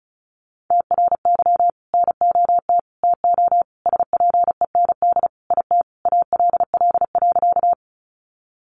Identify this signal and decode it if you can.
Morse code, heard as a keyed tone, about 34 wpm, 700 Hz. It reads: TRY NOT TO SPEND IT ALL.
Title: Sentences from Top 800 Words 35wpm